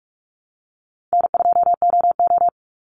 Morse code, heard as a keyed tone, about 44 wpm, 700 Hz. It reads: N2OO
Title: Easy Call Signs - 45wpm